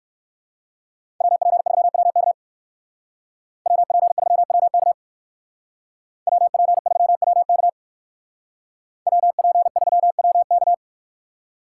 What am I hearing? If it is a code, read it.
Morse code, heard as a keyed tone, about 58 wpm, 700 Hz. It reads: WP3WK WP3WK WP3WK WP3WK